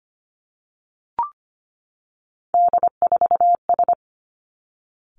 Morse code, heard as a keyed tone, about 25 wpm, 700 Hz. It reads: D4S
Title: Three Challenging Letter-Number Combinations 25wpm